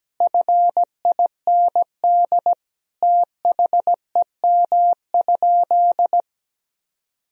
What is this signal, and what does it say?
Morse code, heard as a keyed tone, about 17 wpm, 700 Hz. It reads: FIND THEM?
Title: Sentences from Top 200 Words and Abbreviations 17wpm